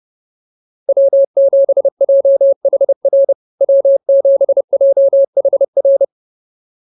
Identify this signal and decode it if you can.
Morse code, heard as a keyed tone, about 30 wpm, 590 Hz. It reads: W7JHR W7JHR